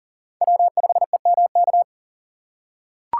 Morse code, heard as a keyed tone, about 40 wpm, 700 Hz. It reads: W5EMK
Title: USA Call Signs - 5 Characters - 2x Character Spacing - 40wpm